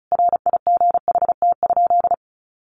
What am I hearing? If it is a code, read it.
Morse code, heard as a keyed tone, about 34 wpm, 700 Hz. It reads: RIGHT?
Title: Sentences from Top 1200 Words 35wpm